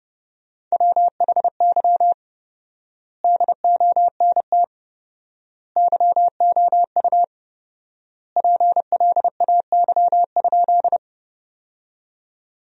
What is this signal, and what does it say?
Morse code, heard as a keyed tone, about 30 wpm, 700 Hz. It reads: WHY DONT YOU PLAY?
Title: 4x Word-Spacing - Sentences from Top 300 Words - 30wpm